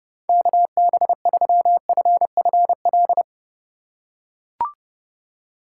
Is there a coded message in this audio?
Yes: Morse code from 0.3 to 3.2 s, heard as a keyed tone, about 30 wpm, 700 Hz. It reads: KB3FFL